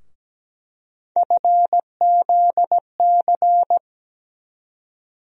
Morse code, heard as a keyed tone, about 17 wpm, 700 Hz. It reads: FZC
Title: USA Call Sign Postfixes - 3 Letters 17wpm